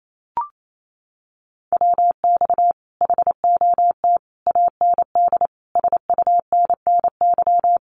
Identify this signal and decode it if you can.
Morse code, heard as a keyed tone, about 28 wpm, 700 Hz. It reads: WX HOT AND SUNNY
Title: Common QSO Phrases Encore 28wpm